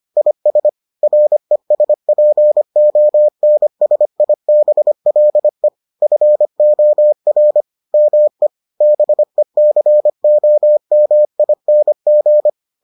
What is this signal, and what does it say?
Morse code, heard as a keyed tone, about 25 wpm, 600 Hz. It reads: IS RESPONSIBLE FOR ME BECOMING